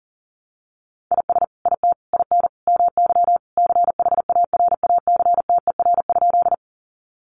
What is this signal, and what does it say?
Morse code, heard as a keyed tone, about 40 wpm, 700 Hz. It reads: IS IT IN MY CHARACTER?
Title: Sentences from Top 1200 Words 40wpm